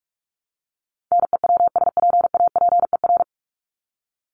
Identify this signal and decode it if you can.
Morse code, heard as a keyed tone, about 44 wpm, 700 Hz. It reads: NEWSPAPER
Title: Top 751-1000 Words 45wpm